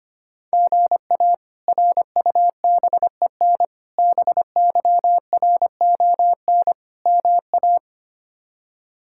Morse code, heard as a keyed tone, about 25 wpm, 700 Hz. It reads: GA RUBEN BYRON MA
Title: K1USN SST QSO Practice - 25wpm